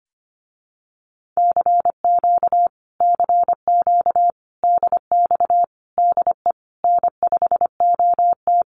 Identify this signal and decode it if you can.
Morse code, heard as a keyed tone, about 25 wpm, 700 Hz. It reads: CQ CQ DX DE N5OT
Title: Common QSO Phrases Encore 25wpm